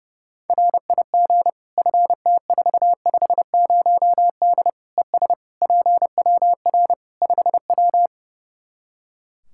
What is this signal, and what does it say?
Morse code, heard as a keyed tone, about 32 wpm, 700 Hz. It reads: RIG FT450D ES PWR 5W
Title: Practice QSO - Exchange 2 - Info - 30wpm